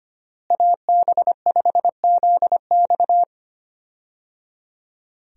Morse code, heard as a keyed tone, about 25 wpm, 700 Hz. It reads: AB5ZX
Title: USA Call Signs - 5 Characters - Repeated 3x - 25wpm